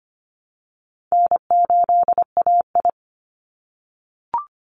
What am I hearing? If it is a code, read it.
Morse code, heard as a keyed tone, about 25 wpm, 700 Hz. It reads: N8AI
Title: Easy Call Signs - 25wpm